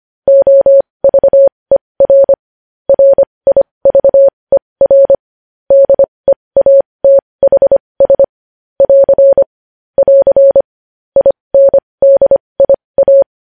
Morse code, heard as a keyed tone, about 25 wpm, spaced as for 20 wpm, 550 Hz. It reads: OVER RIVER DEATHS <AR> <AR> INDIA
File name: podcast / News headlines-1x-25x20